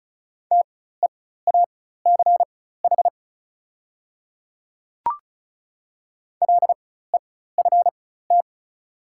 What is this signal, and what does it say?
Morse code, heard as a keyed tone, about 36 wpm, spaced as for 16 wpm, 690 Hz. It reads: TEACH LEFT
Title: 4x Character Spacing - Top 500 Words - 35wpm